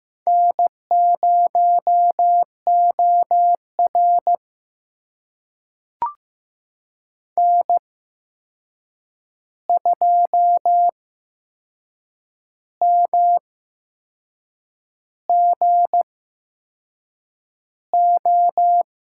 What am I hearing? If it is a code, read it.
Morse code, heard as a keyed tone, about 15 wpm, spaced as for 6 wpm, 700 Hz. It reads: N0OR N2MGO
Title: USA Call Signs - 8x Character Spacing - 15wpm